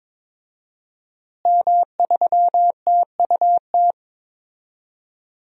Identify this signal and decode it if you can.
Morse code, heard as a keyed tone, about 22 wpm, 700 Hz. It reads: M3TUT
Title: DXCC-100 Call Signs - 5 Characters - Repeated 2x - 22wpm